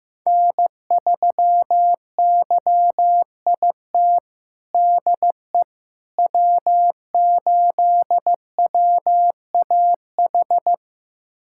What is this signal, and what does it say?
Morse code, heard as a keyed tone, about 15 wpm, 700 Hz. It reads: N3YIT DE W8WAH
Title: USA Callsign DE Callsign - 5 Characters - 15wpm